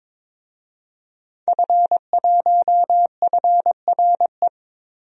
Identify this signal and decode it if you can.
Morse code, heard as a keyed tone, about 22 wpm, 700 Hz. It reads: F1FRE
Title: DXCC-100 Call Signs - 5 Characters - Repeated 3x - 22wpm